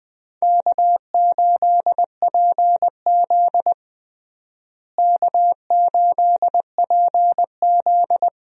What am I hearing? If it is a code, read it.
Morse code, heard as a keyed tone, about 20 wpm, 700 Hz. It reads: K8PZ K8PZ
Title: USA Call Signs - 4 Characters - Repeated 3x - 20wpm